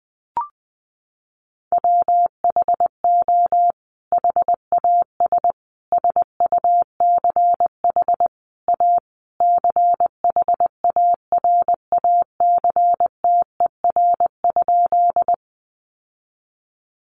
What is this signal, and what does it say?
Morse code, heard as a keyed tone, about 20 wpm, 700 Hz. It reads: WHO HAS SUCH A CHARACTER?
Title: Sentences from Top 1200 Words 20wpm